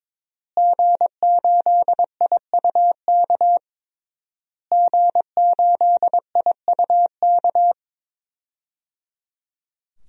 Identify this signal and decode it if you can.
Morse code, heard as a keyed tone, about 22 wpm, 700 Hz. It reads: G8IUK G8IUK
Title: DXCC-100 Call Signs - 5 Characters - Repeated 3x - 22wpm